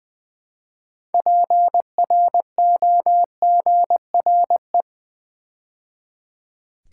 Morse code, heard as a keyed tone, about 20 wpm, 690 Hz. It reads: PROGRE
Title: Build A Word - 3 Letters At A Time - 8-Letter Words - 20wpm